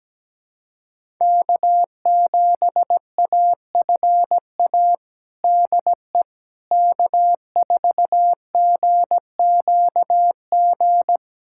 Morse code, heard as a keyed tone, about 17 wpm, 700 Hz. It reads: K7AFA DE K4GQG
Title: USA Callsign DE Callsign - 5 Characters - 17wpm